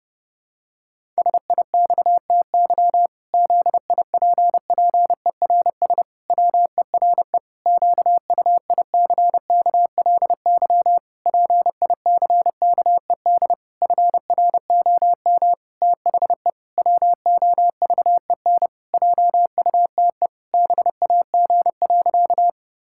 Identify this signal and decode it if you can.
Morse code, heard as a keyed tone, about 28 wpm, 700 Hz. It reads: SIXTY ZIPPERS WERE QUICKLY PICKED FROM THE WOVEN JUTE BAG.
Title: Pangram Challenge v2 - 30wpm